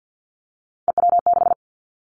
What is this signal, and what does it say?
Morse code, heard as a keyed tone, about 50 wpm, 700 Hz. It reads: EW6
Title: Three Challenging Letter-Number Combinations 50wpm